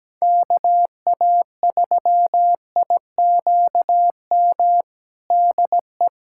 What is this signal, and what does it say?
Morse code, heard as a keyed tone, about 17 wpm, 700 Hz. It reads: KA3IQM DE
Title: USA Callsign DE Callsign - 6 Characters - 17wpm